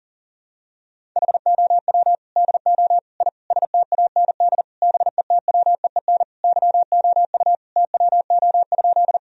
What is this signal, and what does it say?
Morse code, heard as a keyed tone, about 40 wpm, 700 Hz. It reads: HOW DO I STAND BETWEEN YOU TWO?
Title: Sentences from Top 1000 Words - Review Part 1 - 40wpm